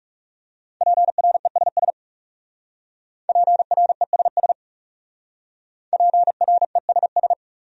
Morse code, heard as a keyed tone, about 37 wpm, 700 Hz. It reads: PRESS PRESS PRESS
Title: Speed-Racing Top 300 Words 30wpm